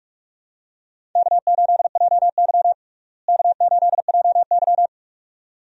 Morse code, heard as a keyed tone, about 46 wpm, 700 Hz. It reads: K8JY K8JY
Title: USA Call Signs - 4 Characters - Repeated 2x - 45wpm